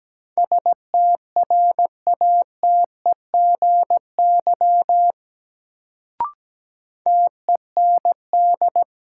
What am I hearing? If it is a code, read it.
Morse code, heard as a keyed tone, about 17 wpm, 700 Hz. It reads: STRATEGY TEND
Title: Sentences from Top 1000 Words - Review Part 5 - New Word Drill - 17wpm